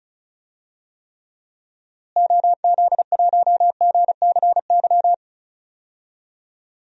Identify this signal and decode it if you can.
Morse code, heard as a keyed tone, about 36 wpm, 700 Hz. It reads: OZ1GCY
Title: DXCC-100 Call Signs - 6 Characters - Repeated 3x - 35wpm